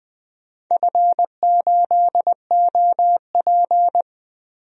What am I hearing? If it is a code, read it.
Morse code, heard as a keyed tone, about 21 wpm, 700 Hz. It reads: F8OP
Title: DXCC-100 Call Signs - 4 Characters - Repeated 3x - 20wpm